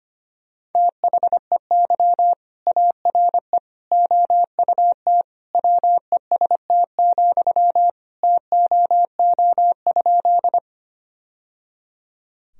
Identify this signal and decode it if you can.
Morse code, heard as a keyed tone, about 25 wpm, 700 Hz. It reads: THEY ARE OUT WEST, TOO?
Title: Sentences from Top 1000 Words - Review Part 2 - 25wpm